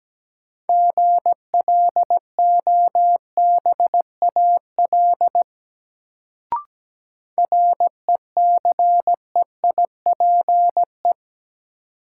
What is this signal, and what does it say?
Morse code, heard as a keyed tone, about 17 wpm, 700 Hz. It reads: GLOBAL RECEIPE
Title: Sentences from Top 1200 Words - New Word Drill - 17wpm